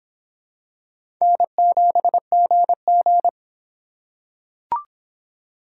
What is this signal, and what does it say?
Morse code, heard as a keyed tone, about 25 wpm, 700 Hz. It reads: N7GG